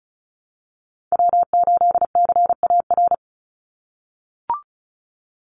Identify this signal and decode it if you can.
Morse code, heard as a keyed tone, about 34 wpm, 700 Hz. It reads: W8CAR
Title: Call Signs - 35wpm